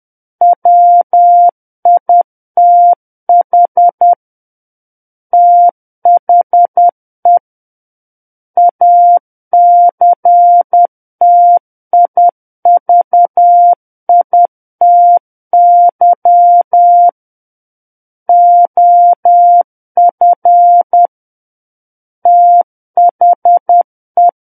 Morse code, heard as a keyed tone, about 10 wpm, 700 Hz. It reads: WITH THE ACTIVITY OF THE
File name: Book One / 01-Chapter 1 - 10 WPM